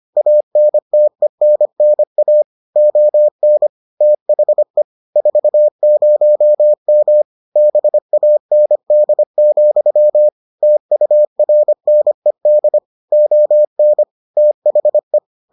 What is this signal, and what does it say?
Morse code, heard as a keyed tone, about 25 wpm, 600 Hz. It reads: ANTENNA ON THE 40M BAND, TURNED ON THE